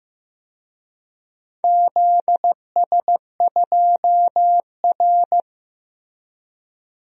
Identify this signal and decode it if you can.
Morse code, heard as a keyed tone, about 15 wpm, 700 Hz. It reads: ZS2R
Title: DXCC-100 Call Signs - 4 Characters - Repeated 3x - 15wpm